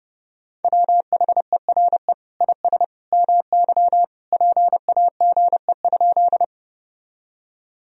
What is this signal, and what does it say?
Morse code, heard as a keyed tone, about 30 wpm, 700 Hz. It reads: WHERE IS MY PAGE?
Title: Sentences from Top 1000 Words - Review Part 1 - 30wpm